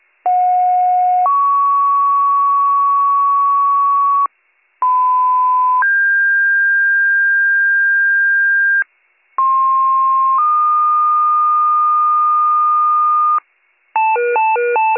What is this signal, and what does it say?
Two-tone page, heard as a one-second tone, then a three-second tone, about 730 then 1120 Hz, about 1010 then 1600 Hz, about 1060 then 1220 Hz. It is heard from 0.2 to 13.4 s.